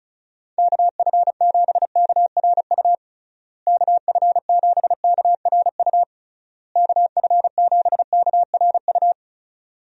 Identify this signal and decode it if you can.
Morse code, heard as a keyed tone, about 36 wpm, 700 Hz. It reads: KF7KRU KF7KRU KF7KRU
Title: USA Call Signs - 6 Characters - Repeated 3x - 35wpm